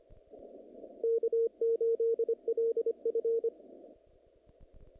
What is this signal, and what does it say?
Morse code, heard as a keyed tone, about 26 wpm, 460 Hz. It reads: K8LF
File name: contests / naqp-cw-k5tr-op-k5pi- -2200